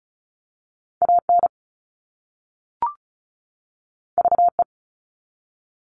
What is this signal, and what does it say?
Morse code, heard as a keyed tone, about 36 wpm, 700 Hz. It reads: AN VE
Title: All Cut Numbers - 2-Digit 35wpm